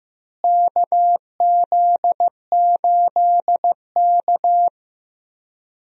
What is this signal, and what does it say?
Morse code, heard as a keyed tone, about 15 wpm, 700 Hz. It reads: KZ8K
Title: USA Call Signs - 4 Characters - Repeated 3x - 15wpm